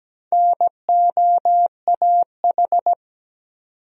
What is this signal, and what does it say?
Morse code, heard as a keyed tone, about 17 wpm, 700 Hz. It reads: NOAH
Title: USA Common Names - 17wpm